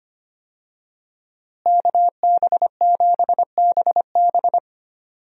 Morse code, heard as a keyed tone, about 25 wpm, 700 Hz. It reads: KB7BB
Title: USA Call Signs - 5 Characters - Repeated 2x - 25wpm